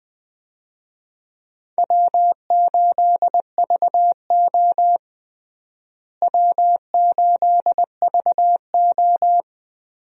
Morse code, heard as a keyed tone, about 20 wpm, 700 Hz. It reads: W8VO W8VO
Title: USA Call Signs - 4 Characters - Repeated 2x - 20wpm